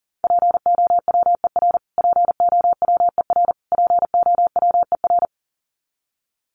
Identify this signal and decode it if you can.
Morse code, heard as a keyed tone, about 40 wpm, 700 Hz. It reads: POWER POWER POWER
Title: Top 500 Words Repeated 40wpm